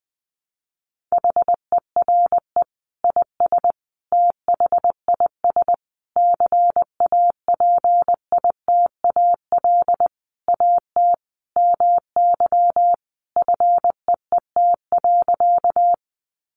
Morse code, heard as a keyed tone, about 20 wpm, 700 Hz. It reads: HERE IS THIS CAPITAL AT MY FEET.
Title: Sentences from Top 1000 Words 20wpm